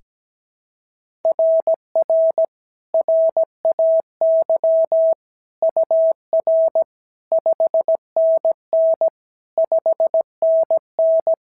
Morse code, heard as a keyed tone, about 16 wpm, 650 Hz. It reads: RR RAY UR 5NN 5NN
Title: SOTA QSO Practice 17wpm